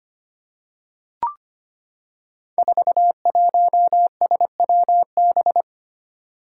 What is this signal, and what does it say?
Morse code, heard as a keyed tone, about 25 wpm, 700 Hz. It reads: 41SWB
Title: Five Challenging Letter-Number Combinations 25wpm